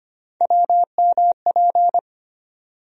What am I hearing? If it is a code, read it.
Morse code, heard as a keyed tone, about 25 wpm, 700 Hz. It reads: WMP
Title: USA Call Sign Postfixes - 3 Letters 25wpm